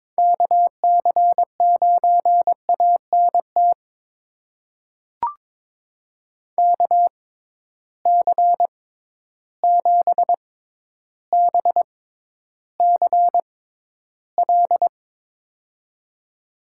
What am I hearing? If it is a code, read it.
Morse code, heard as a keyed tone, about 22 wpm, spaced as for 11 wpm, 700 Hz. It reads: KC9ANT KC7BCL
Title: USA Call Signs - 6 Characters - 6x Character Spacing - 22wpm